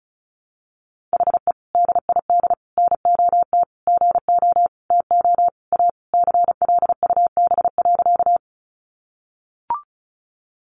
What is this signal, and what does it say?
Morse code, heard as a keyed tone, about 34 wpm, 700 Hz. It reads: HE DID NOT GO TO A CLUB.
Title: Sentences from Top 1200 Words 35wpm